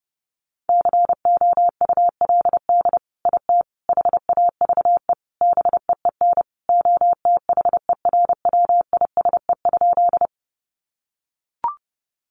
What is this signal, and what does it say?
Morse code, heard as a keyed tone, about 30 wpm, 700 Hz. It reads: COULD IT HAVE BEEN OTHERWISE?
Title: Sentences from Top 1200 Words 30wpm